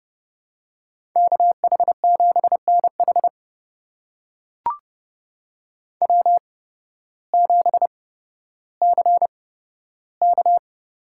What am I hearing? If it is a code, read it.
Morse code, heard as a keyed tone, about 31 wpm, spaced as for 14 wpm, 700 Hz. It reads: KH7NH W7CK
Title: USA Call Signs - 5 Characters - 8x Character Spacing - 30wpm